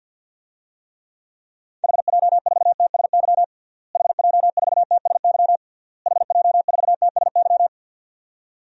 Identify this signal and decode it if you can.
Morse code, heard as a keyed tone, about 52 wpm, 700 Hz. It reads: HJ4TSY HJ4TSY HJ4TSY
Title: DXCC-100 Call Signs - 6 Characters - Repeated 3x - 50wpm